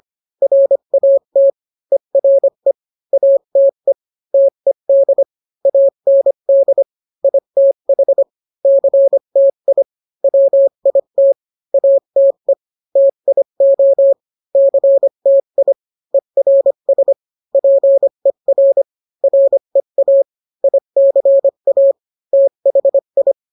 Morse code, heard as a keyed tone, about 25 wpm, 550 Hz. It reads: RAT ERE ATE TED AND ITH CTI WIT ATE TIO CTI ERS PER REA ICA THI